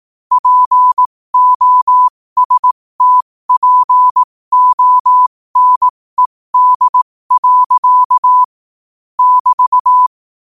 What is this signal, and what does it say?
Morse code, heard as a keyed tone, about 18 wpm, spaced as for 15 wpm, 1000 Hz. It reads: POSTPONED. <BT>
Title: 15 WPM morse code quotes for Fri, 01 Mar 2024